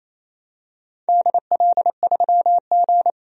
Morse code, heard as a keyed tone, about 27 wpm, 690 Hz. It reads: DL3G
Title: DXCC-100 Call Signs - 4 Characters - 28wpm